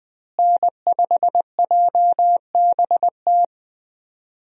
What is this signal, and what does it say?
Morse code, heard as a keyed tone, about 21 wpm, 700 Hz. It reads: N5JBT
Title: USA Call Signs - 5 Characters - Repeated 3x - 20wpm